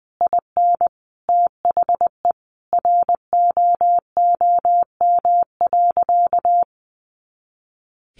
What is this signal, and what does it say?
Morse code, heard as a keyed tone, about 20 wpm, 700 Hz. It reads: IN THE ROOM.
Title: A Sound Of Thunder 20wpm